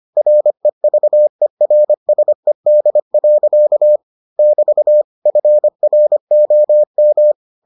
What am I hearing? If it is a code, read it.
Morse code, heard as a keyed tone, about 25 wpm, 600 Hz. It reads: REVERSED. <BT> FROM